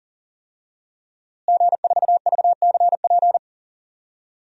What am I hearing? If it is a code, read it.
Morse code, heard as a keyed tone, about 38 wpm, 700 Hz. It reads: G4VCP
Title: DXCC-100 Call Signs - 5 Characters - Repeated 2x - 40wpm